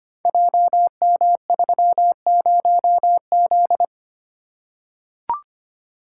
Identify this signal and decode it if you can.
Morse code, heard as a keyed tone, about 25 wpm, 700 Hz. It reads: JM30Z
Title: Code Groups - 5 Letter:Numbers - 25wpm